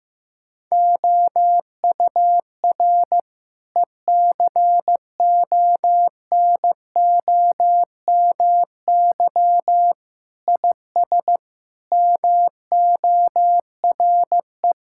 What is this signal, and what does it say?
Morse code, heard as a keyed tone, about 14 wpm, 700 Hz. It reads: OUR ECONOMY IS MORE
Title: Sentences from Top 1000 Words - Review Part 6 - 15wpm